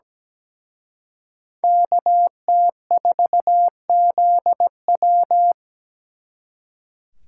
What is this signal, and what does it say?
Morse code, heard as a keyed tone, about 20 wpm, 700 Hz. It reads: KT4ZW
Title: Speed Racing - USA 2x2 Call Signs 17wpm